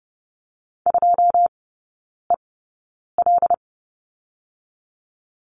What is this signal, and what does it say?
Morse code, heard as a keyed tone, about 30 wpm, 700 Hz. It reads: 2 E L
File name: Lesson 19e - Letter w - Three Characters 30wpm